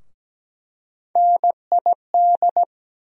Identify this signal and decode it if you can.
Morse code, heard as a keyed tone, about 17 wpm, 700 Hz. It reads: NID
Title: USA Call Sign Postfixes - 3 Letters 17wpm